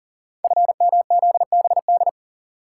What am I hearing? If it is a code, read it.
Morse code, heard as a keyed tone, about 39 wpm, 700 Hz. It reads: FMZBD
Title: Code Groups - 5 Letters - 40wpm